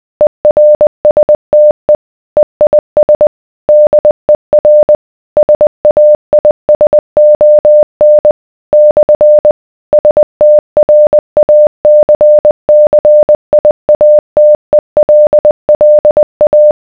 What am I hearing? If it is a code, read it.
Morse code, heard as a keyed tone, about 20 wpm, 600 Hz. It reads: ERSTE EIS DER SAISON / STRACCIATELLA